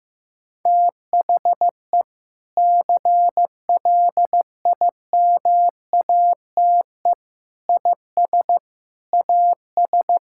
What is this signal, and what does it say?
Morse code, heard as a keyed tone, about 16 wpm, 700 Hz. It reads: THE CLIMATE IS AS
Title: Sentences from Top 1600 Words 15wpm